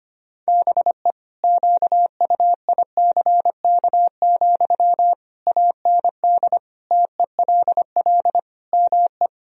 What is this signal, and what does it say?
Morse code, heard as a keyed tone, about 25 wpm, 700 Hz. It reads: BE QUICK, AND TELL ME
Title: Sentences from Top 1000 Words - Review Part 3 - 25wpm